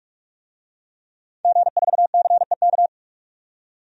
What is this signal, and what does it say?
Morse code, heard as a keyed tone, about 46 wpm, 700 Hz. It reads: G4CEK
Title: DXCC-100 Call Signs - 5 Characters - 45wpm